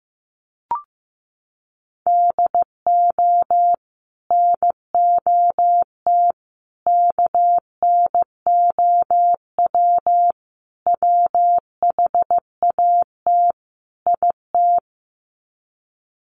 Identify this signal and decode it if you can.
Morse code, heard as a keyed tone, about 15 wpm, 700 Hz. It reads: DO NOT KNOW WHAT IT
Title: Sets of 5 Words 15wpm